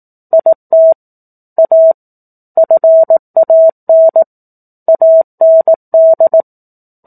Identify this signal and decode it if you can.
Morse code, heard as a keyed tone, about 18 wpm, 650 Hz. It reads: IT A FAN AND
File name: 18 WPM